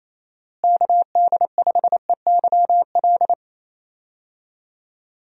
Morse code, heard as a keyed tone, about 28 wpm, 700 Hz. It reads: KD5EYL
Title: USA Call Signs - 6 Characters - Repeated 3x - 28wpm